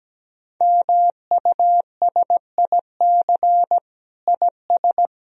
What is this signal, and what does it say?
Morse code, heard as a keyed tone, about 17 wpm, 700 Hz. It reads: MUSIC IS
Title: Sentences from Top 1000 Words - Review Part 1 - 17wpm